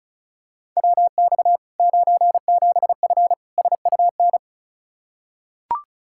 Morse code, heard as a keyed tone, about 34 wpm, 700 Hz. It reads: WX 97F SUN
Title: Practice QSO - Exchange 2 - Info - 35wpm